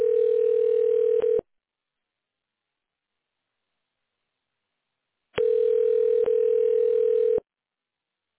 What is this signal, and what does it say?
Ringback tone (North America), 440 + 480 Hz.